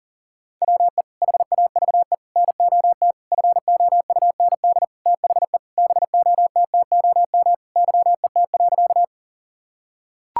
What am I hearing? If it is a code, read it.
Morse code, heard as a keyed tone, about 37 wpm, 700 Hz. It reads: WE HAVE NOT FOUND THE BOTTOM YET.
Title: Sentences from Top 1000 Words - Review Part 6 - 40wpm